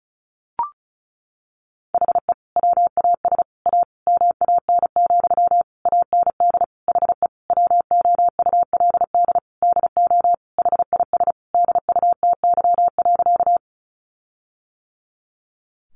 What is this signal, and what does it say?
Morse code, heard as a keyed tone, about 34 wpm, 700 Hz. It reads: HE WAS A MAN, AND HE WOULD DO HIS DUTY.
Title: Sentences from Top 1500 Words 35wpm